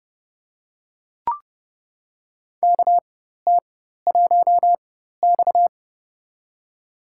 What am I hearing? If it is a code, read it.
Morse code, heard as a keyed tone, about 32 wpm, spaced as for 14 wpm, 700 Hz. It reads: KT1X
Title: USA Call Signs - 4x Character Spacing - 30wpm